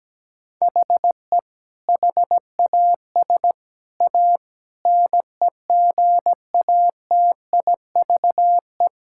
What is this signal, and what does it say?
Morse code, heard as a keyed tone, about 17 wpm, 700 Hz. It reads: HE HAS A NEGATIVE
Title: Sentences from Top 1000 Words - Review Part 6 - 17wpm